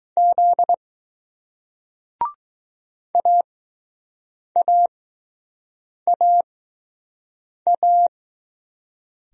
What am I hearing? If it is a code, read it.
Morse code, heard as a keyed tone, about 20 wpm, spaced as for 6 wpm, 700 Hz. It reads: Z AAAA